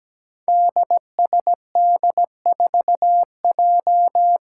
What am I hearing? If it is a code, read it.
Morse code, heard as a keyed tone, about 17 wpm, 700 Hz. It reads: DSD4J
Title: Five Challenging Letter-Number Combinations 17wpm